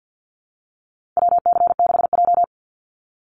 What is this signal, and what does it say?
Morse code, heard as a keyed tone, about 50 wpm, 700 Hz. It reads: WC6J
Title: NA SOTA Chaser Callsigns 50wpm